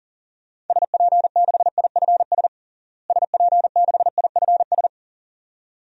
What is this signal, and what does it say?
Morse code, heard as a keyed tone, about 40 wpm, 700 Hz. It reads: SP6IFS SP6IFS
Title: DXCC-100 Call Signs - 6 Characters - Repeated 2x - 40wpm